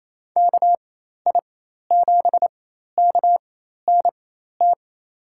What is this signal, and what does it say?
Morse code, heard as a keyed tone, about 28 wpm, spaced as for 13 wpm, 700 Hz. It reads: KI7KNT
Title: USA Call Signs - 6 Characters - 4x Character Spacing - 28wpm